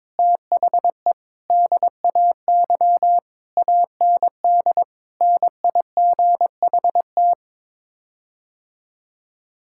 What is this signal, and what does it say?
Morse code, heard as a keyed tone, about 22 wpm, 700 Hz. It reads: THE DAY AND NIGHT
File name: practice / 22wpm / Sets of 8 Words - 22wpm